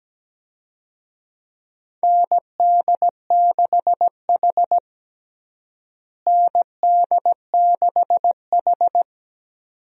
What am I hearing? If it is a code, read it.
Morse code, heard as a keyed tone, about 17 wpm, 700 Hz. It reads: ND6H ND6H
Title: USA Call Signs - 4 Characters - Repeated 2x - 17wpm